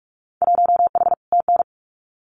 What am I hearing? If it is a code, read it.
Morse code, heard as a keyed tone, about 46 wpm, 700 Hz. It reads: 1H TN
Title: Winter Field Day Exchanges 45wpm